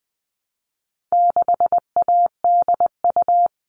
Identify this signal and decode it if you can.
Morse code, heard as a keyed tone, about 20 wpm, 700 Hz. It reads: 6ADU
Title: All Cut Numbers - 4-Digit 20wpm